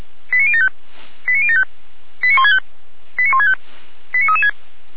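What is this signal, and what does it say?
Five-tone selective call (ZVEI), heard as quick runs of five tones, tones of about 70 ms. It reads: 89075 89075 89165 89165 89336
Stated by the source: ZVEI five-tone selective call received by SDR, from the audio archive of Feuerwehr Orlamünde.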